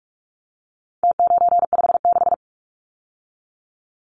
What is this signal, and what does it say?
Morse code, heard as a keyed tone, about 44 wpm, 700 Hz. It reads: T956
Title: Numbers - Contest Style - 4-Digit 45wpm